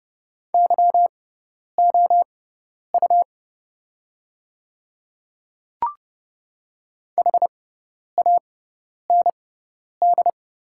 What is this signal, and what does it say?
Morse code, heard as a keyed tone, about 29 wpm, spaced as for 10 wpm, 700 Hz. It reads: YOU HAND
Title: 6x Character Spacing - Top 300 Words - 30wpm